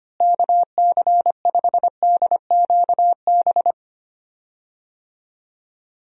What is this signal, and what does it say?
Morse code, heard as a keyed tone, about 25 wpm, 700 Hz. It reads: KC5DQB
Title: USA Call Signs - 6 Characters - Repeated 3x - 25wpm